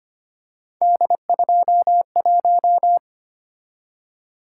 Morse code, heard as a keyed tone, about 25 wpm, 700 Hz. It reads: D21
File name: Three Letter-Number Combinations - 25wpm